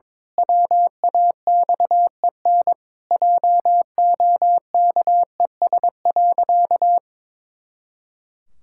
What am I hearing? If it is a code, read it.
Morse code, heard as a keyed tone, about 22 wpm, 700 Hz. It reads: WAXEN JOKES.
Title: Pangram Challenge v2 - 22wpm